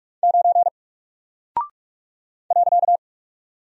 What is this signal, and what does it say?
Morse code, heard as a keyed tone, about 44 wpm, 700 Hz. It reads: 9 .
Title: Single Character (Letter:Number:Punctuation:Prosign) 45wpm